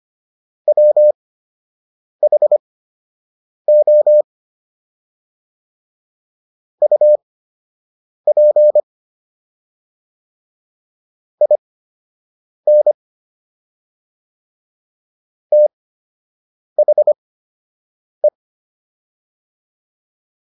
Morse code, heard as a keyed tone, about 25 wpm, spaced as for 7 wpm, 600 Hz. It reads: WHO UP IN THE